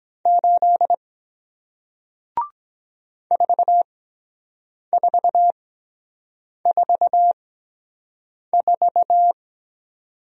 Morse code, heard as a keyed tone, about 23 wpm, spaced as for 7 wpm, 700 Hz. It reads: 8 4444